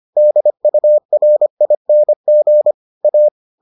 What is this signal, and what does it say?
Morse code, heard as a keyed tone, about 25 wpm, 600 Hz. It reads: DURING A